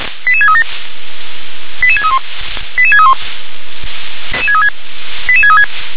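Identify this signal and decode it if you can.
Five-tone selective call (ZVEI), heard as quick runs of five tones, tones of about 70 ms. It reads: 88637 88631 88631 88636
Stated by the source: ZVEI five-tone selective call 88634 received by SDR.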